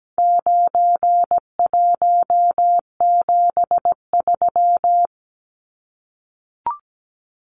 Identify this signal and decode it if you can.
Morse code, heard as a keyed tone, about 17 wpm, 700 Hz. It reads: 9173
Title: Numbers - 4-Digit 17wpm